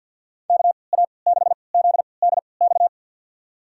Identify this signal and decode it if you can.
Morse code, heard as a keyed tone, about 51 wpm, spaced as for 28 wpm, 690 Hz. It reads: KA6ZDX
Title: USA Call Signs - 6 Characters - 3x Character Spacing - 50wpm